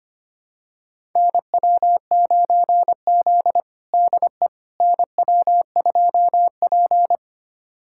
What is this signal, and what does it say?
Morse code, heard as a keyed tone, about 25 wpm, 700 Hz. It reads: NW9Z DE NW2P
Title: USA Callsign DE Callsign - 4 Characters - 25wpm